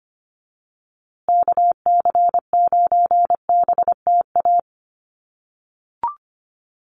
Morse code, heard as a keyed tone, about 25 wpm, 700 Hz. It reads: KC9BTA
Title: USA Call Signs Repeated 2x 25wpm